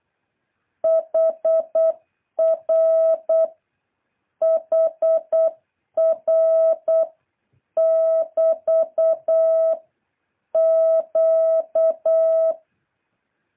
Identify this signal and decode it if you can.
Morse code, heard as a keyed tone, about 8 wpm, 660 Hz. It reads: HR HR <BT> Q
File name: jh8cbh / 2023cw